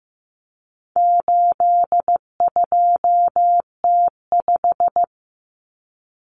Morse code, heard as a keyed tone, about 15 wpm, 700 Hz. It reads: 82T5
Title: Common Cut Numbers - 4-Digit 15wpm